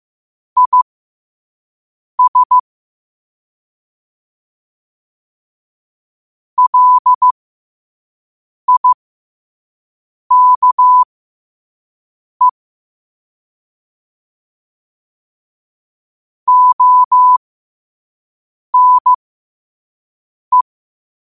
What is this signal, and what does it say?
Morse code, heard as a keyed tone, about 15 wpm, spaced as for 5 wpm, 1000 Hz. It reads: IS LIKE ONE
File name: qotdfiles / 5wpm